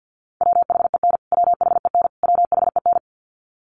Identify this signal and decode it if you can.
Morse code, heard as a keyed tone, about 50 wpm, 700 Hz. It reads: WHEN WHEN WHEN
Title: Top 200 Words Repeated 50wpm